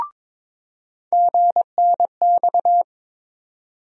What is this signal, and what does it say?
Morse code, heard as a keyed tone, about 22 wpm, 700 Hz. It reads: GNX
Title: USA Call Sign Postfixes - 3 Letters 22wpm